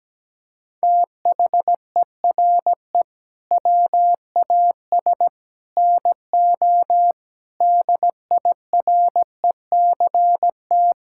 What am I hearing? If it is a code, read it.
Morse code, heard as a keyed tone, about 17 wpm, 700 Hz. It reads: THERE WAS NO DIRECT
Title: Sentences from Top 1000 Words - Review Part 6 - 17wpm